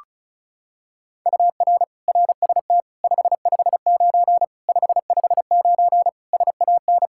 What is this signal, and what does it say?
Morse code, heard as a keyed tone, about 36 wpm, 700 Hz. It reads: UR RST 559 559 SAN
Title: Practice QSO - Exchange 1 - Beginning - 35wpm